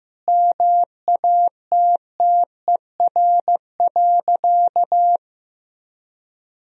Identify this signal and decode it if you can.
Morse code, heard as a keyed tone, about 16 wpm, 700 Hz. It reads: MATTER.
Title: Sentences from Top 1000 Words - Review Part 3 - 15wpm